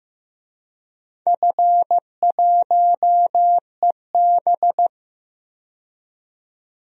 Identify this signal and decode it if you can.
Morse code, heard as a keyed tone, about 15 wpm, 700 Hz. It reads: F1EB
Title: DXCC-100 Call Signs - 4 Characters - 15wpm